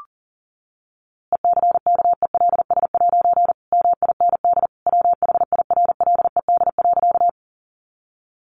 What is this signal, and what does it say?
Morse code, heard as a keyed tone, about 40 wpm, 700 Hz. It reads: ECKELS' MIND WHIRLED.
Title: A Sound Of Thunder 40wpm